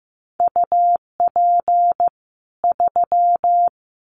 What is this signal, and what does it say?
Morse code, heard as a keyed tone, about 15 wpm, 700 Hz. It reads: UP 3
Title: Common QSO Phrases 15wpm